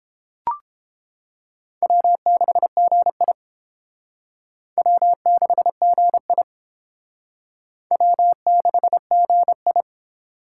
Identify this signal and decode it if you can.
Morse code, heard as a keyed tone, about 29 wpm, 700 Hz. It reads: W6GI W6GI W6GI